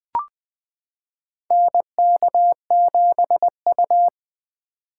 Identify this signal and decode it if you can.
Morse code, heard as a keyed tone, about 20 wpm, 700 Hz. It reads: NK7U
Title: K1USN SST QSO Practice - 20wpm